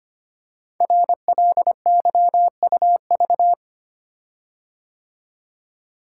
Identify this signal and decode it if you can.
Morse code, heard as a keyed tone, about 25 wpm, 700 Hz. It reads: RLYUV
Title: Code Groups - 5 Letters - 25wpm